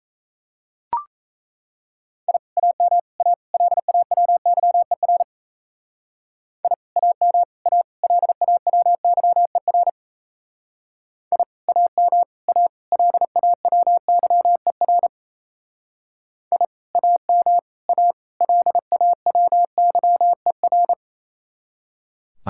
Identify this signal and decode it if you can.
Morse code, heard as a keyed tone, about 36 wpm, 700 Hz. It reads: I AM A LAWYER I AM A LAWYER I AM A LAWYER I AM A LAWYER